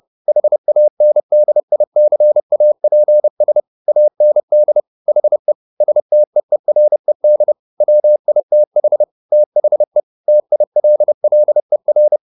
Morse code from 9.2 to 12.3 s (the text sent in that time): THE TILLER